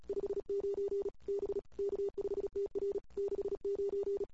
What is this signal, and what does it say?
Morse code, heard as a keyed tone, about 36 wpm, 410 Hz. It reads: 59 B K5TR 69
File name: CW / 2008-ss-cw-k5tr-op-n5rz -21:00